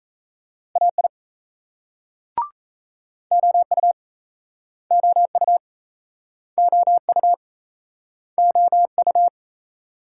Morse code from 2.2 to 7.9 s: OU OU OU

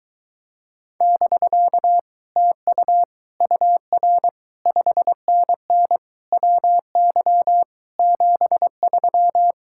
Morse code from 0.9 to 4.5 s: <BK> TU UR

